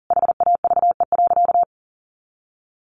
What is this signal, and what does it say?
Morse code, heard as a keyed tone, about 40 wpm, 700 Hz. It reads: HAVE.
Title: Sentences from Top 1200 Words 40wpm